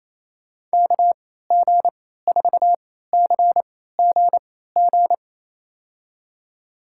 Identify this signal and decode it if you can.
Morse code, heard as a keyed tone, about 28 wpm, spaced as for 16 wpm, 700 Hz. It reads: KG4CGG
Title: USA Call Signs - 3x Character Spacing - 28wpm